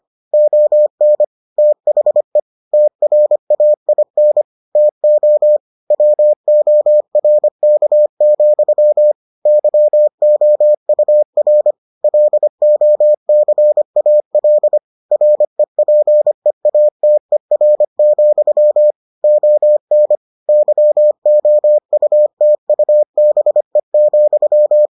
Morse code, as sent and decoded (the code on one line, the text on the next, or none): --- -. / - .... / - .-. .- .. -. / - --- / .-- --- .-. -.- --..-- / -.-- --- ..- .-. / .-.. --- -.-. .- .-.. / .-. .--. .- - . .-. --..-- / --- -. / -.-- --- ..- - ..- -... --..--
ON THE TRAIN TO WORK, YOUR LOCAL REPEATER, ON YOUTUBE,